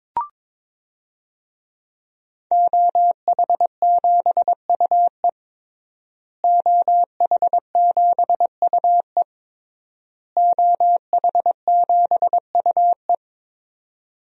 Morse code, heard as a keyed tone, about 22 wpm, 700 Hz. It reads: OH7UE OH7UE OH7UE
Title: DXCC-100 Call Signs - 5 Characters - Repeated 3x - 22wpm